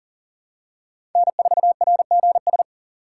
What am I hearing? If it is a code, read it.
Morse code, heard as a keyed tone, about 40 wpm, 700 Hz. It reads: N4RGS
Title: USA Call Signs - 5 Characters - 6x Character Spacing - 40wpm